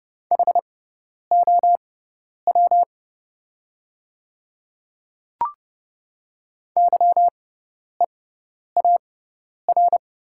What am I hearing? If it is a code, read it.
Morse code, heard as a keyed tone, about 30 wpm, spaced as for 10 wpm, 700 Hz. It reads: HOW YEAR
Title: 6x Character Spacing - Top 200 Words - 30wpm